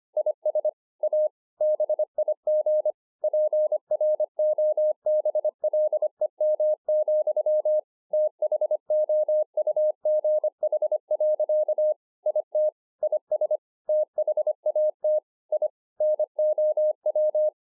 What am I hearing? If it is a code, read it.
Morse code, heard as a keyed tone, about 25 wpm, 610 Hz. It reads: IS A BIG PROBLEM, THOUGH. IT IS THAT I NOW
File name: 2011-04-mp3 / THIS IS NOT MY SHACK 2 25WPM